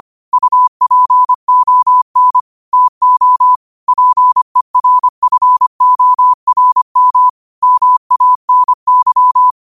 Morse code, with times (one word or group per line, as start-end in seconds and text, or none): 0.3-2.4 UPON
2.7-3.5 TO
3.9-7.3 PERFORM
7.6-9.5 MANY